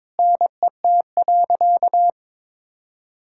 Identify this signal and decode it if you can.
Morse code, heard as a keyed tone, about 22 wpm, 700 Hz. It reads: NET.
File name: Sentences with 9-Word Phrases - 22wpm